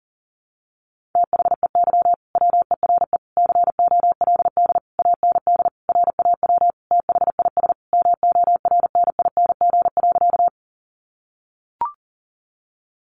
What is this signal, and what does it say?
Morse code, heard as a keyed tone, about 40 wpm, 700 Hz. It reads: THEY WERE COLD AND RAW THIS MORNING.
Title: Sentences from Top 1500 Words 40wpm